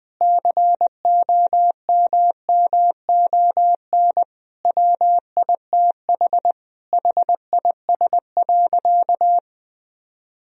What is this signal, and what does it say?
Morse code, heard as a keyed tone, about 20 wpm, 700 Hz. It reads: COMMON WITH HIS.
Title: Sentences from Top 1000 Words - Review Part 3 - 20wpm